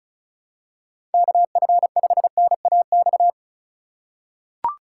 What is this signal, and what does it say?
Morse code, heard as a keyed tone, about 34 wpm, 700 Hz. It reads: KF5NAX
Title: USA Call Signs - 6 Characters - 4x Character Spacing - 35wpm